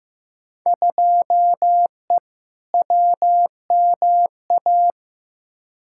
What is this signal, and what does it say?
Morse code, heard as a keyed tone, about 15 wpm, 700 Hz. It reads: 2E WMA
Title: Field Day Exchanges Encore - 15wpm